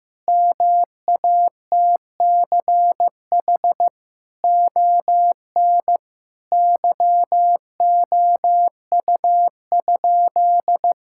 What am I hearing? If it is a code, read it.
Morse code, heard as a keyed tone, about 15 wpm, 700 Hz. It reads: MATCH ON YOU?
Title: Sentences from Top 1600 Words 15wpm